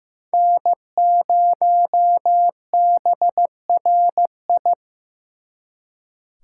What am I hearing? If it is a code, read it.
Morse code, heard as a keyed tone, about 18 wpm, 700 Hz. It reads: N0BRI
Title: Speed Racing - USA Call Signs 15wpm